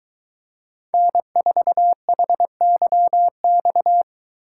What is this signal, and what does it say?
Morse code, heard as a keyed tone, about 22 wpm, 700 Hz. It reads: N4HYX